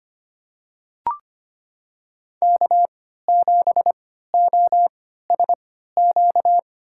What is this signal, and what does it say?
Morse code, heard as a keyed tone, about 25 wpm, spaced as for 14 wpm, 700 Hz. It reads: K7OSQ
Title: USA Call Signs - 5 Characters - 3x Character Spacing - 25wpm